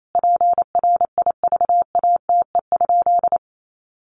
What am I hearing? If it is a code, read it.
Morse code, heard as a keyed tone, about 28 wpm, 700 Hz. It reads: PRIVATE?
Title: Sentences from Top 1000 Words 28wpm